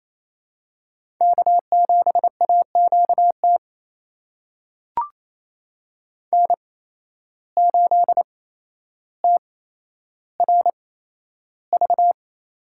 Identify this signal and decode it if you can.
Morse code, heard as a keyed tone, about 28 wpm, spaced as for 12 wpm, 700 Hz. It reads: K7AQT N8TRV
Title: USA Call Signs - 5 Characters - 8x Character Spacing - 28wpm